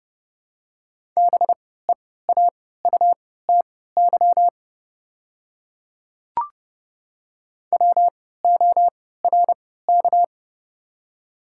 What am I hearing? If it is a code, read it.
Morse code, heard as a keyed tone, about 30 wpm, spaced as for 17 wpm, 700 Hz. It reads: BEAUTY WORK
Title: 3x Character Spacing - Top 500 Words - 30wpm